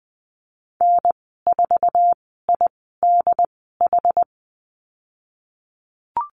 Morse code, heard as a keyed tone, about 20 wpm, spaced as for 15 wpm, 700 Hz. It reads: N4IDH
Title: 193 Lesson 31g - Letter v - Call Signs 20wpm